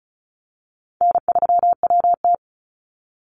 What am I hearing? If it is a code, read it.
Morse code, heard as a keyed tone, about 34 wpm, 700 Hz. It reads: N3WT
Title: Easy Call Signs Repeated 3x - 35wpm